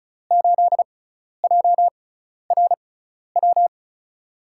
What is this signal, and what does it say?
Morse code, heard as a keyed tone, about 36 wpm, spaced as for 12 wpm, 700 Hz. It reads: 8JRW
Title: USA Call Signs - 6 Characters - 6x Character Spacing - 35wpm